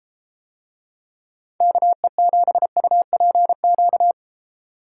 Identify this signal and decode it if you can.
Morse code, heard as a keyed tone, about 33 wpm, 700 Hz. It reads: KE7UPQ